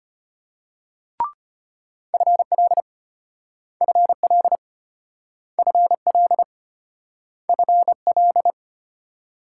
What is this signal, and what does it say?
Morse code, heard as a keyed tone, about 34 wpm, 700 Hz. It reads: FL FL FL FL